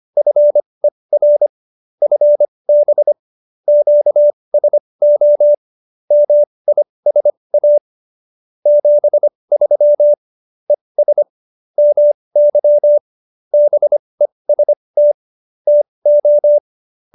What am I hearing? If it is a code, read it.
Morse code, heard as a keyed tone, about 25 wpm, spaced as for 20 wpm, 590 Hz. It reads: FER FB QSO MISA 73 ES MY BEST TO